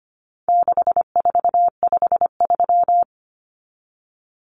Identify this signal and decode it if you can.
Morse code, heard as a keyed tone, about 25 wpm, 700 Hz. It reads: 6453
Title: Numbers - 4-Digit 25wpm